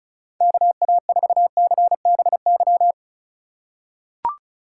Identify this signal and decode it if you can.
Morse code, heard as a keyed tone, about 36 wpm, 690 Hz. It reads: KA4CBY
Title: USA Call Signs - 6 Characters - 6x Character Spacing - 35wpm